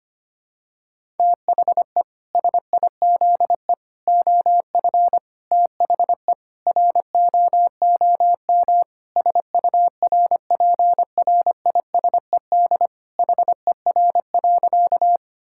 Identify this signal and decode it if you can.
Morse code, heard as a keyed tone, about 25 wpm, 700 Hz. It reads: THE SIZE OF THE ROOM SURPRISED HER.
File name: practice / 25wpm / Sentences with 4-Word Phrases - 25wpm